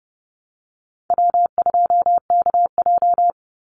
Morse code, heard as a keyed tone, about 30 wpm, 700 Hz. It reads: W2KJ
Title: Call Signs - 30wpm